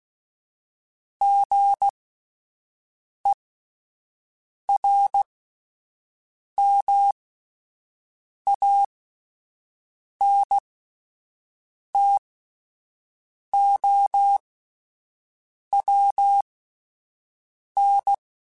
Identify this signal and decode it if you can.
Morse code, heard as a keyed tone, about 16 wpm, spaced as for 5 wpm, 800 Hz. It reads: GERMANTOWN
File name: fromSMs / FromSMs mp3s / Cities@5